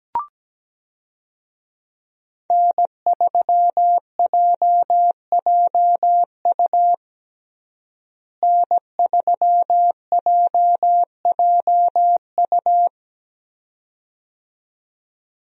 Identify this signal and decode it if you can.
Morse code, heard as a keyed tone, about 17 wpm, 700 Hz. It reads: N3JJU N3JJU
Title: USA Call Signs - 5 Characters - Repeated 2x - 17wpm